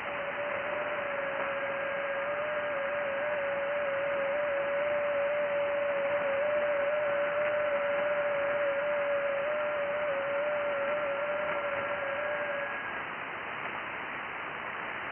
FT8.
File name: IC-7100